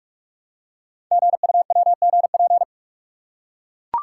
Morse code, heard as a keyed tone, about 44 wpm, 700 Hz. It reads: GUWGP
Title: Code Groups - 5 Letters - 45wpm